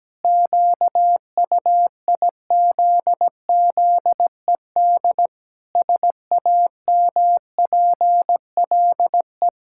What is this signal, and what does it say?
Morse code, heard as a keyed tone, about 17 wpm, 700 Hz. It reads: QUIZZED SAMPLE
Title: Pangram Challenge v2 - 17wpm